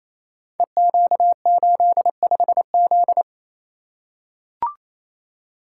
Morse code, heard as a keyed tone, about 29 wpm, 700 Hz. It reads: EQ85Z
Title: Code Groups - 5 Letter:Numbers - 28wpm